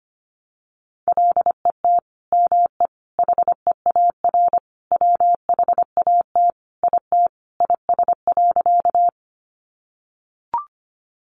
Morse code, heard as a keyed tone, about 25 wpm, 700 Hz. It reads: LET ME HEAR WHAT IT IS.
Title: Sentences from Top 300 Words 25wpm